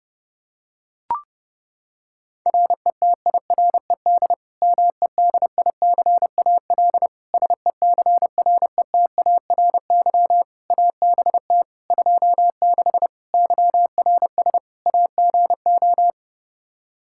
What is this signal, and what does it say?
Morse code, heard as a keyed tone, about 32 wpm, 700 Hz. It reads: RETIRED MEDICAL SECRETARY ABT 26 YRS AGO
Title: K1USN SST QSO Practice 30wpm